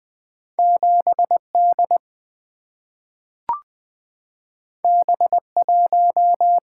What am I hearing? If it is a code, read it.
Morse code, heard as a keyed tone, about 20 wpm, 700 Hz. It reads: 7D B1
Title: Single-Letter Single-Number 20wpm